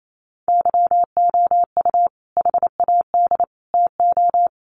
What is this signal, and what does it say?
Morse code, heard as a keyed tone, about 28 wpm, 700 Hz. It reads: YOU HAD TO
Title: Sentences from Top 600 Words 28wpm